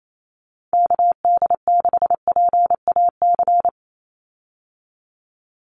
Morse code, heard as a keyed tone, about 28 wpm, 700 Hz. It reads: KD6PAC
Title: USA Call Signs 28wpm